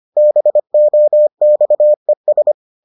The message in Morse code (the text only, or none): BOXES